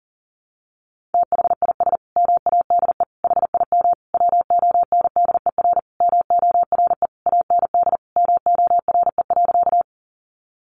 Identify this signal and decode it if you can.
Morse code, heard as a keyed tone, about 40 wpm, 700 Hz. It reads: THIS MADE HIM WONDER MORE AND MORE.
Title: Sentences from Top 500 Words 40wpm